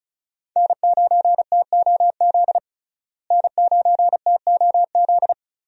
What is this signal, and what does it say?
Morse code, heard as a keyed tone, about 36 wpm, 700 Hz. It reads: N9TOZ N9TOZ
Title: USA Call Signs - 5 Characters - Repeated 3x - 35wpm